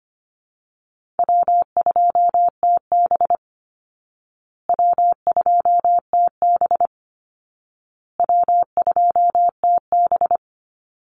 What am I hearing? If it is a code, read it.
Morse code, heard as a keyed tone, about 25 wpm, 700 Hz. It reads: W2TB W2TB W2TB
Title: Call Signs Repeated 2x - 25wpm